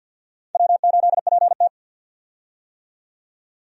Morse code, heard as a keyed tone, about 52 wpm, 700 Hz. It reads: W8PT
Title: USA Call Signs - 4 Characters - 50wpm